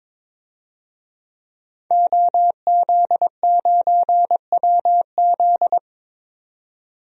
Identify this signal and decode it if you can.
Morse code, heard as a keyed tone, about 22 wpm, 700 Hz. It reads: OZ9WZ
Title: DXCC-100 Call Signs - 5 Characters - 22wpm